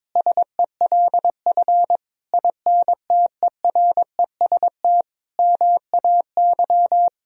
Morse code, heard as a keyed tone, about 22 wpm, 700 Hz. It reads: SELF INTEREST MAY
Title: Sentences from Top 1000 Words - Review Part 2 - 22wpm